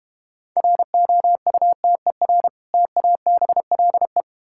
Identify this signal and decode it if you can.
Morse code, heard as a keyed tone, about 30 wpm, 700 Hz. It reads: ROUTER TABLE
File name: practice / 25wpm / Difficult Words - Speed-Racing with Hints - 25wpm